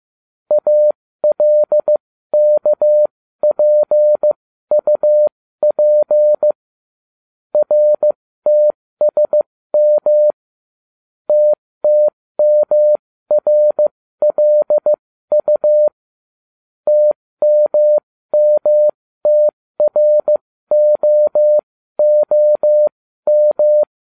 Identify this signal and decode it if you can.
Morse code, heard as a keyed tone, about 15 wpm, spaced as for 13 wpm, 600 Hz. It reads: ALKPUP RTSM TTMRLU TMMTROOM